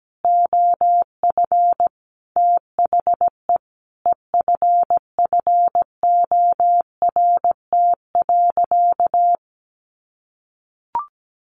Morse code, heard as a keyed tone, about 17 wpm, 700 Hz. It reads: OF THE EFFORT.
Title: Sentences from Top 1200 Words 17wpm